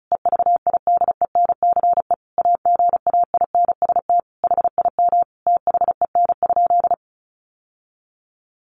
Morse code, heard as a keyed tone, about 34 wpm, 700 Hz. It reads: EVIDENCE AGAINST HIM THEN?
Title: Sentences from Top 1300 Words 35wpm